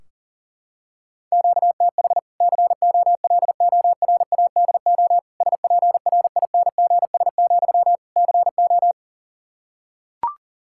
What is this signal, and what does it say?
Morse code, heard as a keyed tone, about 40 wpm, 700 Hz. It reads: QTH COLORADO SPRINGS, CO
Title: Practice QSO - Exchange 1 - Beginning - 40wpm